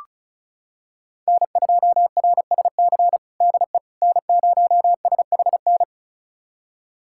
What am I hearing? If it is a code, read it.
Morse code, heard as a keyed tone, about 34 wpm, 700 Hz. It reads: N2RSC DE N0SHN
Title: USA Callsign DE Callsign - 5 Characters - 35wpm